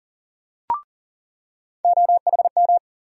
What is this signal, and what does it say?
Morse code, heard as a keyed tone, about 40 wpm, 700 Hz. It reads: OHM
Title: USA Call Sign Postfixes - 3 Letters 40wpm